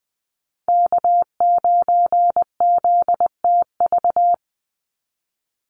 Morse code, heard as a keyed tone, about 20 wpm, 700 Hz. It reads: K9ZTV
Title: Call Signs Repeated 2x - 20wpm